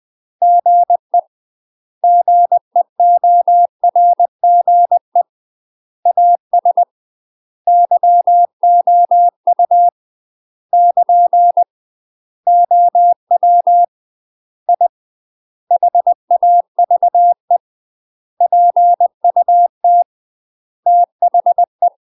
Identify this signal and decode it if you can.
Morse code, heard as a keyed tone, about 20 wpm, 700 Hz. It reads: GE GEORGE AS YOU <KN> OW I HAVE PUT THE
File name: UK MASTER QSOs TRAINING / 20WPM / UK MASTER QSOs 0026